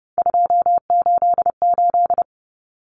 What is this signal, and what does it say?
Morse code, heard as a keyed tone, about 30 wpm, 700 Hz. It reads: 288
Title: Common Cut Numbers - 3-Digit 30wpm